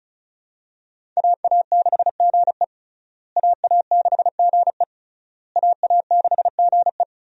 Morse code, heard as a keyed tone, about 36 wpm, 700 Hz. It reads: AA6GE AA6GE AA6GE
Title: USA Call Signs - 5 Characters - Repeated 3x - 35wpm